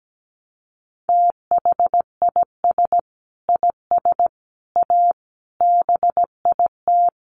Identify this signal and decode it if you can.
Morse code, heard as a keyed tone, about 17 wpm, 700 Hz. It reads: THIS IS A BIT
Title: Sentences from Top 900 Words 17wpm